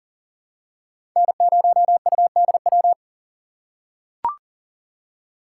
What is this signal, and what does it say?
Morse code, heard as a keyed tone, about 40 wpm, 700 Hz. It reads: N0UDW
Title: USA Call Signs - 5 Characters - 6x Character Spacing - 40wpm